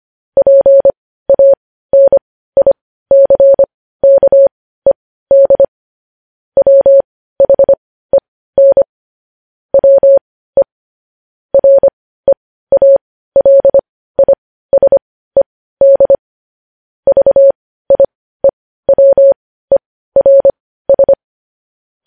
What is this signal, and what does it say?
Morse code, heard as a keyed tone, about 25 wpm, spaced as for 15 wpm, 550 Hz. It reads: PANICKED WHEN WE REALISED VIEWERS